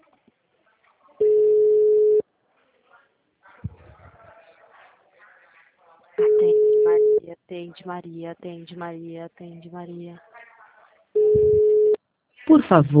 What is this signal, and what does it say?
Ringback tone (Europe, 425 Hz), with speech.